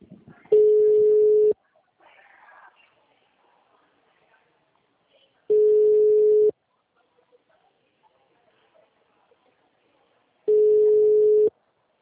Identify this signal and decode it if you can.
Ringback tone (Europe, 425 Hz), with speech.